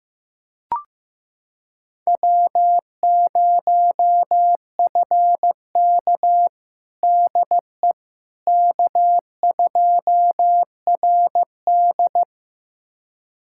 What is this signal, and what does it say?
Morse code, heard as a keyed tone, about 16 wpm, 700 Hz. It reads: W0FK DE K2RD
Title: USA Callsign DE Callsign - 4 Characters - 15wpm